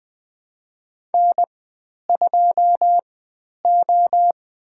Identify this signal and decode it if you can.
Morse code, heard as a keyed tone, about 20 wpm, spaced as for 10 wpm, 700 Hz. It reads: N2O
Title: 102 Lesson 17g - Letter u - Call Signs 20wpm